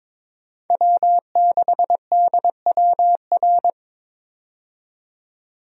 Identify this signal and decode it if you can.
Morse code, heard as a keyed tone, about 22 wpm, 700 Hz. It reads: W6DWR
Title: USA Call Signs - 5 Characters - 22wpm